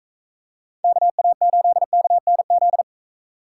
Morse code, heard as a keyed tone, about 43 wpm, 700 Hz. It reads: KA8KNZ